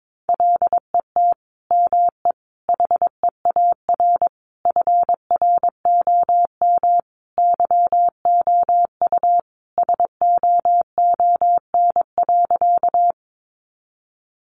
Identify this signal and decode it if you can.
Morse code, heard as a keyed tone, about 22 wpm, 700 Hz. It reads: LET ME HEAR FROM YOU SOON.
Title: Sentences from Top 400 Words 22wpm